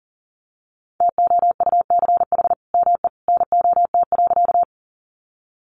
Morse code, heard as a keyed tone, about 40 wpm, 700 Hz. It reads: TOUCH ME NOT.
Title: Sentences from Top 1200 Words 40wpm